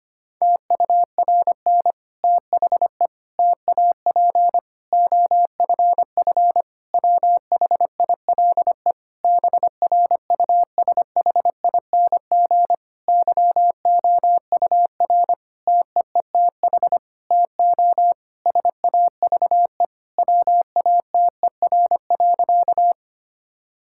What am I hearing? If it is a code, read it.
Morse code, heard as a keyed tone, about 25 wpm, 700 Hz. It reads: TURN THE TAP OFF WHILE BRUSHING YOUR TEETH TO SAVE WATER.
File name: practice / 25wpm / Difficult Sentences with Hints - 25wpm